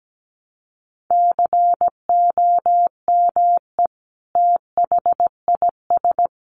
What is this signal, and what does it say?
Morse code, heard as a keyed tone, about 17 wpm, 700 Hz. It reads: COME THIS
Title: Sentences from Top 100 Words 17wpm